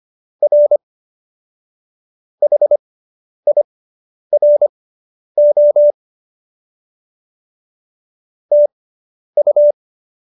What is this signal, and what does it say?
Morse code, heard as a keyed tone, about 25 wpm, spaced as for 10 wpm, 590 Hz. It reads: R HIRO TU